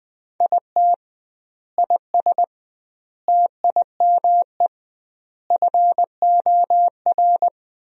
Morse code, heard as a keyed tone, about 21 wpm, 700 Hz. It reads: IT IS TIME FOR
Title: 2x Word-Spacing - Sentences from Top 500 Words - 20wpm